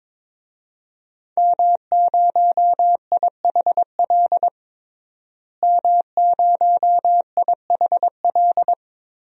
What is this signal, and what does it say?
Morse code, heard as a keyed tone, about 22 wpm, 700 Hz. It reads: M0IHL M0IHL
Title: DXCC-100 Call Signs - 5 Characters - Repeated 3x - 22wpm